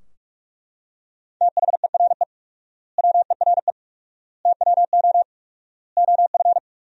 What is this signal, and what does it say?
Morse code, heard as a keyed tone, about 42 wpm, 700 Hz. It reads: THERE WERE TWO OF
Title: 4x Word-Spacing - Sentences from Top 100 Words - 45wpm